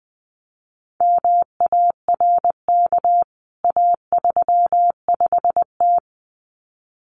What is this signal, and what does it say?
Morse code, heard as a keyed tone, about 20 wpm, 700 Hz. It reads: MARK A35T
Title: CWT Exchange 20wpm